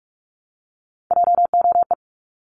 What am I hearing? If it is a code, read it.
Morse code, heard as a keyed tone, about 46 wpm, 700 Hz. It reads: JOE
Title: Multi-letter Phonemes Repeated 3x 45wpm